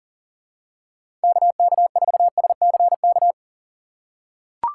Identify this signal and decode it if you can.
Morse code, heard as a keyed tone, about 40 wpm, 700 Hz. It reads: KK4SCK
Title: USA Call Signs - 6 Characters - 4x Character Spacing - 40wpm